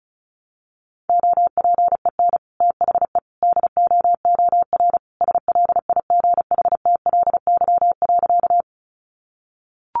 Morse code, heard as a keyed tone, about 34 wpm, 700 Hz. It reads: OPEN THE DOOR SLIGHTLY.
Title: Sentences from Top 1300 Words 35wpm